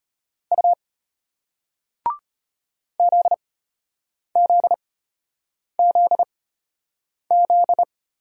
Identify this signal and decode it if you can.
Morse code, heard as a keyed tone, about 33 wpm, spaced as for 8 wpm, 700 Hz. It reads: U ZZZZ